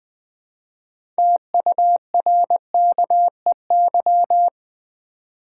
Morse code, heard as a keyed tone, about 20 wpm, 700 Hz. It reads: TURKEY
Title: Easy Words with Hints - 20wpm